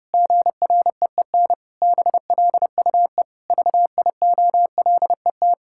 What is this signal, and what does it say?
Morse code, heard as a keyed tone, about 30 wpm, 700 Hz. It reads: GREEN BLUE VIOLET
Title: Pangram Challenge v2 - 30wpm